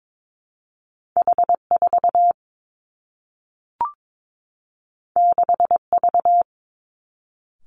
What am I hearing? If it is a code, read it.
Morse code, heard as a keyed tone, about 22 wpm, 700 Hz. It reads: H 4 6 V
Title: Two Challenging Letter-Number Combinations 22wpm